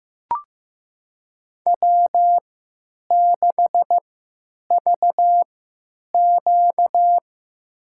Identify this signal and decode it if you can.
Morse code, heard as a keyed tone, about 15 wpm, spaced as for 9 wpm, 700 Hz. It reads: W6VQ
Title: USA Call Signs - 4 Characters - 3x Character Spacing - 15wpm